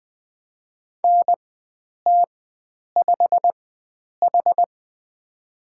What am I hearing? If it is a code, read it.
Morse code, heard as a keyed tone, about 20 wpm, spaced as for 9 wpm, 700 Hz. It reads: NT5H
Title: USA Call Signs - 4 Characters - 4x Character Spacing - 20wpm